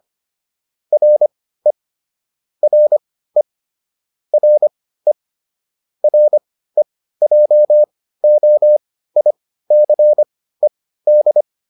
Morse code, heard as a keyed tone, about 25 wpm, spaced as for 15 wpm, 600 Hz. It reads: RE RE RE REJOICED